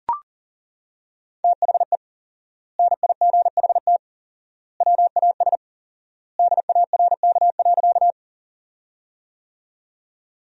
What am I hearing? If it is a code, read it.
Morse code, heard as a keyed tone, about 40 wpm, 700 Hz. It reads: THE NIGHT WAS DARK.
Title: 4x Word-Spacing - Sentences from Top 500 Words - 40wpm